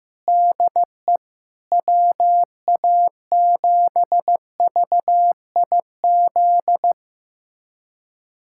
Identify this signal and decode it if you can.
Morse code, heard as a keyed tone, about 15 wpm, 700 Hz. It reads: DE WA7VIZ
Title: USA Callsign DE Callsign - 6 Characters - 15wpm